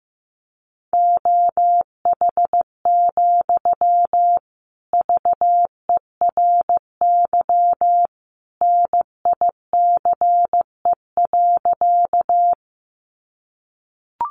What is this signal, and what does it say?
Morse code, heard as a keyed tone, about 15 wpm, 700 Hz. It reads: OH, VERY NICE.
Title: Sentences from Top 1100 Words 15wpm